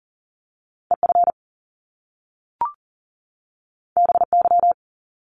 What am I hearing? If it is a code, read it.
Morse code, heard as a keyed tone, about 38 wpm, 700 Hz. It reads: EF BY
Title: Top Two Letter Combinations in Words 40wpm